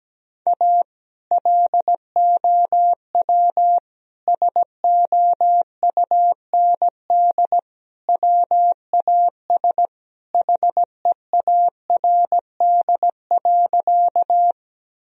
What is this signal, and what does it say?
Morse code, heard as a keyed tone, about 17 wpm, 700 Hz. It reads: A LOW SOUND WAS HEARD.
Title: Sentences from Top 1000 Words - Review Part 1 - 17wpm